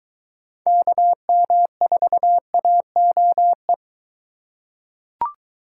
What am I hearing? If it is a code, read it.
Morse code, heard as a keyed tone, about 23 wpm, 700 Hz. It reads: KM4AOE